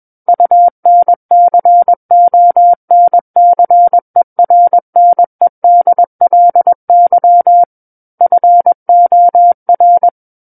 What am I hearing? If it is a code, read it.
Morse code, heard as a keyed tone, about 21 wpm, 700 Hz. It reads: UNCONCERNEDLY FOR
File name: Book One / 12-Chapter 12 - 21 WPM